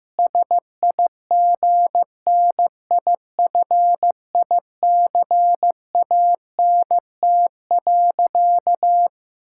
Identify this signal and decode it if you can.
Morse code, heard as a keyed tone, about 16 wpm, 700 Hz. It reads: SIGNIFICANT.
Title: Sentences from Top 1000 Words - Review Part 6 - 15wpm